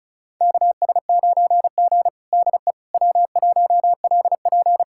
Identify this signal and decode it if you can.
Morse code, heard as a keyed tone, about 36 wpm, 700 Hz. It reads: KS9G DE W1LP
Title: USA Callsign DE Callsign - 4 Characters - 35wpm